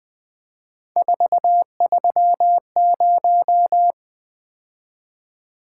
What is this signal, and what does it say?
Morse code, heard as a keyed tone, about 20 wpm, 700 Hz. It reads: 430
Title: Three Letter-Number Combinations - 20wpm